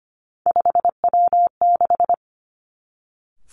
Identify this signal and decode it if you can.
Morse code, heard as a keyed tone, about 25 wpm, 700 Hz. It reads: 5W6
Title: Three Challenging Letter-Number Combinations 25wpm